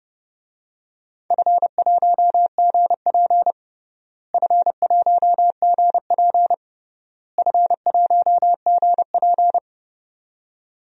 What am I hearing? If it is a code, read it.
Morse code, heard as a keyed tone, about 31 wpm, 700 Hz. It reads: F1GP F1GP F1GP
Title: DXCC-100 Call Signs - 4 Characters - Repeated 3x - 30wpm